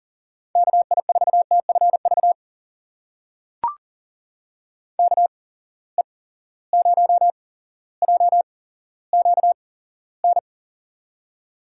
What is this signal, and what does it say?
Morse code, heard as a keyed tone, about 40 wpm, spaced as for 17 wpm, 700 Hz. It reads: KI4TFV KE0JQN
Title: USA Call Signs - 6 Characters - 8x Character Spacing - 40wpm